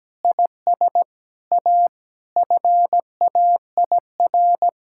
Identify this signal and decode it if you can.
Morse code, heard as a keyed tone, about 17 wpm, 700 Hz. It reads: IS A FAIR
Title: Sentences from Top 1600 Words 17wpm